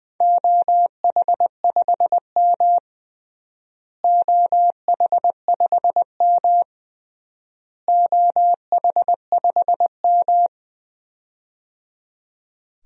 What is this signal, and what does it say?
Morse code, heard as a keyed tone, about 21 wpm, 700 Hz. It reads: OH5M OH5M OH5M
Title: DXCC-100 Call Signs - 4 Characters - Repeated 3x - 20wpm